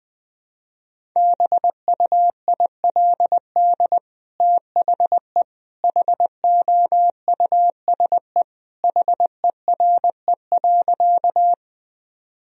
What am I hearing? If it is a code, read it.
Morse code, heard as a keyed tone, about 21 wpm, 700 Hz. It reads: BUILD THE HOUSE HERE.
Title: Sentences from Top 1000 Words - Review Part 1 - 20wpm